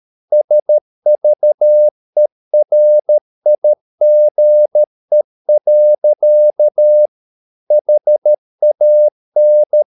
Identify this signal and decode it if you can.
Morse code, heard as a keyed tone, about 13 wpm, 600 Hz. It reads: SVERIGE. HAN
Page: Ferie 13 wpm